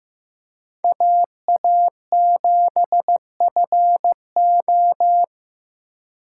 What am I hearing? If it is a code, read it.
Morse code, heard as a keyed tone, about 18 wpm, 700 Hz. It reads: AA7FO
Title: Speed Racing - USA 2x2 Call Signs 15wpm